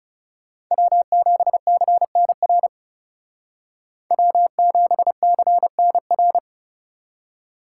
Morse code, heard as a keyed tone, about 36 wpm, 700 Hz. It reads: W7CNR W7CNR
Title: Speed Racing - USA 1x3 Call Signs 30wpm